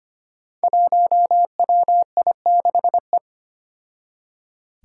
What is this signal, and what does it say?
Morse code, heard as a keyed tone, about 25 wpm, 700 Hz. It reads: 1WI6E
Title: Five Challenging Letter-Number Combinations 25wpm